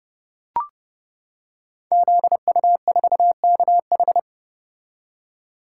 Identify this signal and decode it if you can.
Morse code, heard as a keyed tone, about 30 wpm, 700 Hz. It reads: ZU4KH
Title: Code Groups - 5 Letter:Numbers - 30wpm